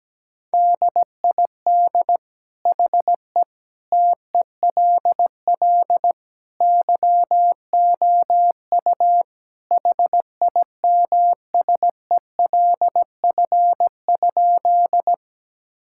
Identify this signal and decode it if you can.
Morse code, heard as a keyed tone, about 17 wpm, 700 Hz. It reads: DID HE TELL YOU HIMSELF?
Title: Sentences from Top 1600 Words 17wpm